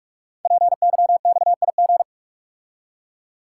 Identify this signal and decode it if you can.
Morse code, heard as a keyed tone, about 44 wpm, 690 Hz. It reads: PYXIG
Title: Code Groups - 5 Letters - 45wpm